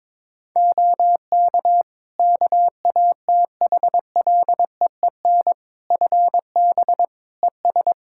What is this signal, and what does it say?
Morse code, heard as a keyed tone, about 22 wpm, 700 Hz. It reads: OK KATHLEEN FB ES
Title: Practice QSO - Exchange 2 - Info - 22wpm